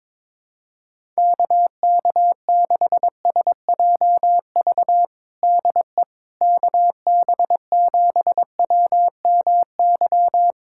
Morse code, heard as a keyed tone, about 22 wpm, 700 Hz. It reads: KK6SJV DE KB7WMY
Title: USA Callsign DE Callsign - 6 Characters - 22wpm